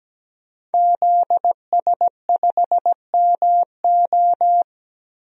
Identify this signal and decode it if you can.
Morse code, heard as a keyed tone, about 17 wpm, 700 Hz. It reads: ZS5MO
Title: DXCC-100 Call Signs - 5 Characters - 17wpm